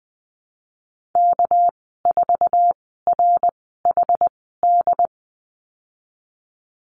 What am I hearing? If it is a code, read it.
Morse code, heard as a keyed tone, about 20 wpm, spaced as for 15 wpm, 700 Hz. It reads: K4RHD
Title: 200 Lesson 32g - Letter k - Call Signs 20wpm